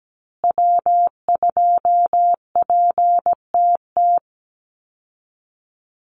Morse code, heard as a keyed tone, about 17 wpm, 700 Hz. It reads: W2PTT
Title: Call Signs Repeated 3x - 17wpm